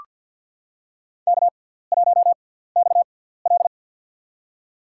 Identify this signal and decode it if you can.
Morse code, heard as a keyed tone, about 45 wpm, spaced as for 17 wpm, 700 Hz. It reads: K1XL
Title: USA Call Signs - 4 Characters - 6x Character Spacing - 50wpm